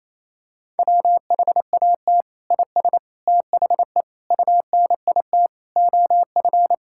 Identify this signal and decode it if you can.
Morse code, heard as a keyed tone, about 28 wpm, 700 Hz. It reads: WHAT IS THE UNIT OF
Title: Sentences from Top 1000 Words - Review Part 6 - 28wpm